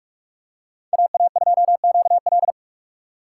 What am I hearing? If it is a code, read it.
Morse code, heard as a keyed tone, about 44 wpm, 700 Hz. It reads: AA2QL
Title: USA Call Signs - 5 Characters - 3x Character Spacing - 45wpm